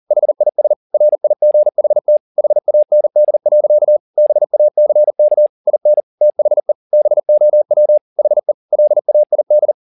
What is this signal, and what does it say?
Morse code, heard as a keyed tone, about 40 wpm, 600 Hz. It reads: HIS RIGHT HAND. BACK IN THE BOW HE LAID
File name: the old man and the sea 2 40wpm